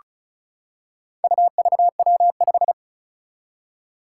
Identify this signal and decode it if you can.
Morse code, heard as a keyed tone, about 36 wpm, 700 Hz. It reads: UVW5
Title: Four Challenging Letter-Number Combinations 35wpm